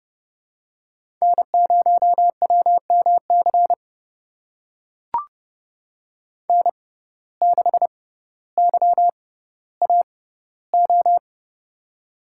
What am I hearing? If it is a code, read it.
Morse code, heard as a keyed tone, about 30 wpm, spaced as for 15 wpm, 700 Hz. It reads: N0WMC N6YAO
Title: USA Call Signs - 5 Characters - 6x Character Spacing - 30wpm